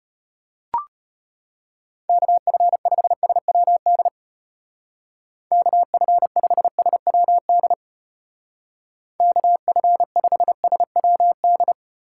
Morse code, heard as a keyed tone, about 33 wpm, 700 Hz. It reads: KF5SWD KF5SWD KF5SWD